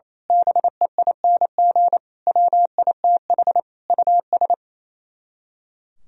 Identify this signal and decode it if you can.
Morse code, heard as a keyed tone, about 27 wpm, 700 Hz. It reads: BEING WITH US
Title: Speed-Racing - Sets of 5 Words 28wpm